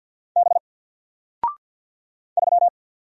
Morse code, heard as a keyed tone, about 48 wpm, 700 Hz. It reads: B 3
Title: Single Character (Letter:Number:Punctuation:Prosign) Alt 50wpm